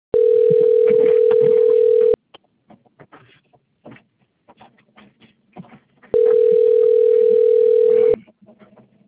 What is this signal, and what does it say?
Ringback tone (North America), 440 + 480 Hz.